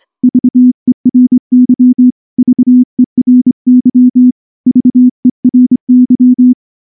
Morse code, heard as a keyed tone, about 24 wpm, 250 Hz. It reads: VERY VERY VERY
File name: morse / morse-training-25WPM